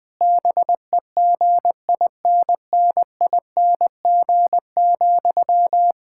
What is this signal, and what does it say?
Morse code, heard as a keyed tone, about 21 wpm, 700 Hz. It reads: BEGINNING,
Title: Sentences from Top 1000 Words - Review Part 6 - 20wpm